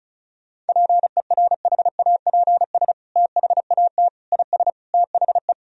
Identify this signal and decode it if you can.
Morse code, heard as a keyed tone, about 36 wpm, 700 Hz. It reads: PERHAPS THAT IS THE
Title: Sentences from Top 1000 Words - Review Part 4 - 35wpm